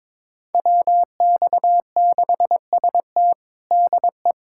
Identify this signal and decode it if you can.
Morse code, heard as a keyed tone, about 21 wpm, 700 Hz. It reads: WX6ST DE
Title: USA Callsign DE Callsign - 5 Characters - 22wpm